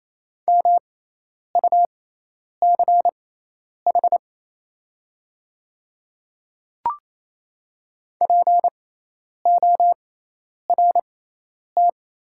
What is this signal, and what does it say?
Morse code, heard as a keyed tone, about 28 wpm, spaced as for 10 wpm, 700 Hz. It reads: MUCH PORT
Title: 6x Character Spacing - Top 200 Words - 28wpm